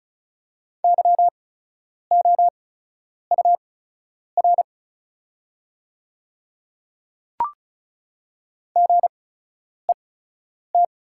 Morse code, heard as a keyed tone, about 34 wpm, spaced as for 10 wpm, 690 Hz. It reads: YOUR GET
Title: 8x Character Spacing - Top 200 Words - 35wpm